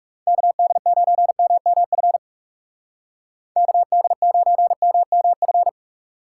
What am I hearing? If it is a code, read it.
Morse code, heard as a keyed tone, about 38 wpm, 700 Hz. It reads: KD9MMF KD9MMF
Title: Speed Racing - USA 2x3 Call Signs 30wpm